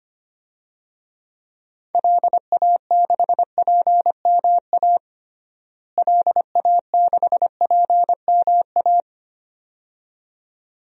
Morse code, heard as a keyed tone, about 25 wpm, 700 Hz. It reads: LA6PMA LA6PMA
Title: DXCC-100 Call Signs - 6 Characters - Repeated 2x - 25wpm